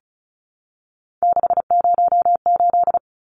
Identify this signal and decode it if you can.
Morse code, heard as a keyed tone, about 34 wpm, 700 Hz. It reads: 608
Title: Numbers - 3-Digit 35wpm